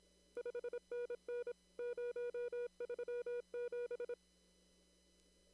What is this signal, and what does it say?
Morse code, heard as a keyed tone, about 27 wpm, 480 Hz. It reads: 5NN 037
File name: CW / kt5j wpx cw-op-k5pi